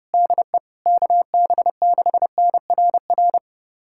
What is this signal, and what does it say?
Morse code, heard as a keyed tone, about 30 wpm, 700 Hz. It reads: DE KB6NRR
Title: USA Callsign DE Callsign 30wpm